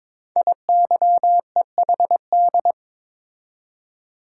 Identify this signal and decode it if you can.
Morse code, heard as a keyed tone, about 22 wpm, 700 Hz. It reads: IYEHD
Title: Code Groups - 5 Letters - 22wpm